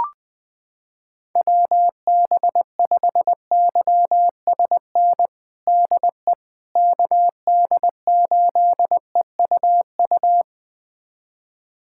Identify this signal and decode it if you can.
Morse code, heard as a keyed tone, about 20 wpm, 700 Hz. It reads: WB5YSN DE KD8EUU
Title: USA Callsign DE Callsign - 6 Characters - 20wpm